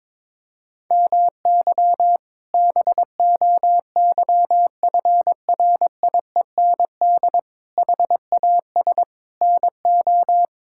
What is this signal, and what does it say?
Morse code, heard as a keyed tone, about 22 wpm, 700 Hz. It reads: MY BOYFRIEND HAS NO
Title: Sentences from Top 1000 Words - Review Part 5 - 22wpm